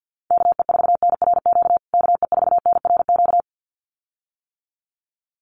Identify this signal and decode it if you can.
Morse code, heard as a keyed tone, about 49 wpm, 700 Hz. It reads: KE4NRQ KE4NRQ
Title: USA Call Signs Repeated 2x 50wpm